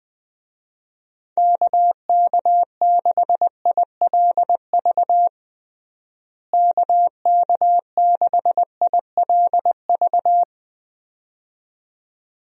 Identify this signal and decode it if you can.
Morse code, heard as a keyed tone, about 20 wpm, 700 Hz. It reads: KK6ILV KK6ILV
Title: USA Call Signs - 6 Characters - Repeated 2x - 20wpm